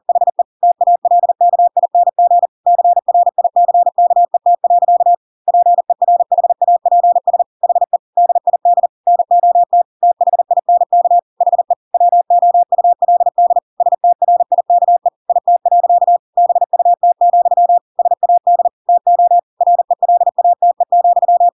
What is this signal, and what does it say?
Morse code, heard as a keyed tone, about 40 wpm, 700 Hz. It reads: HE TALKING CRICKET. PERHAPS HE DID NOT THINK HE WOULD STRIKE IT. BUT, SAD TO RELATE,
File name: Pinocchio / Pinocchio ch4 40 40 ik1hge